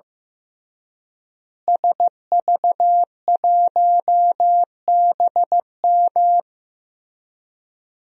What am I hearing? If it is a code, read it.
Morse code, heard as a keyed tone, about 15 wpm, 700 Hz. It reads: SV1BM
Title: DXCC-100 Call Signs - 5 Characters - Repeated 3x - 15wpm